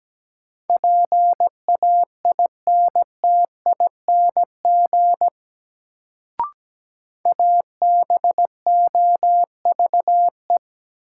Morse code, heard as a keyed tone, about 17 wpm, 700 Hz. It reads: PAINTING ABOVE
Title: Sentences from Top 1200 Words - New Word Drill Alt - 17wpm